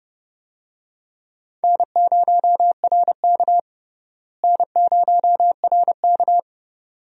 Morse code, heard as a keyed tone, about 30 wpm, 700 Hz. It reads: N0RK N0RK
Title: USA Call Signs - 4 Characters - Repeated 2x - 30wpm